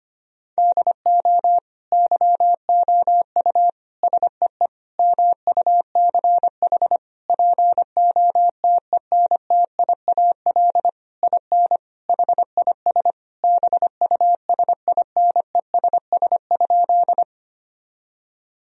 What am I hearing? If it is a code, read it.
Morse code, heard as a keyed tone, about 25 wpm, 700 Hz. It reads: DO YOU SEE MUCH POTENTIAL IN HIS BUSINESS?
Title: Sentences from Top 1000 Words - Review Part 4 - 25wpm